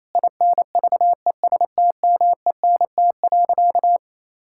Morse code, heard as a keyed tone, about 28 wpm, 700 Hz. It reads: INVESTMENT.
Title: Sentences from Top 1000 Words - Review Part 5 - 28wpm